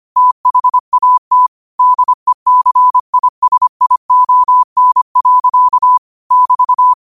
Morse code, heard as a keyed tone, about 25 wpm, 1000 Hz. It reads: THAT DECISION. <BT>
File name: qotdfiles / 25wpm